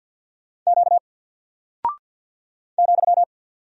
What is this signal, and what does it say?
Morse code, heard as a keyed tone, about 49 wpm, 700 Hz. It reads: Q ,
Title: Single Character (Letter:Number:Punctuation:Prosign) 50wpm